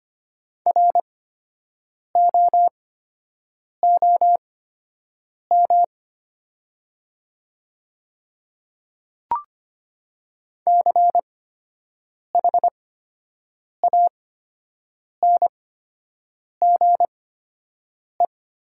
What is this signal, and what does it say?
Morse code, heard as a keyed tone, about 25 wpm, spaced as for 7 wpm, 700 Hz. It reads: ROOM CHANGE
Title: 8x Character Spacing - Top 300 Words - 25wpm